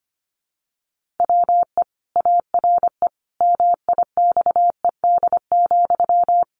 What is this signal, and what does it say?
Morse code, heard as a keyed tone, about 25 wpm, 700 Hz. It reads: WE ARE MIXED,
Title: Sentences from Top 1500 Words 25wpm